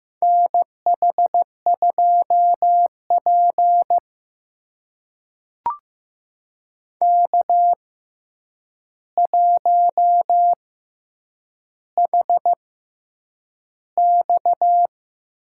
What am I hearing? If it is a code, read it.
Morse code, heard as a keyed tone, about 15 wpm, spaced as for 8 wpm, 700 Hz. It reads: NH2P K1HX
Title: USA Call Signs - 4 Characters - 6x Character Spacing - 15wpm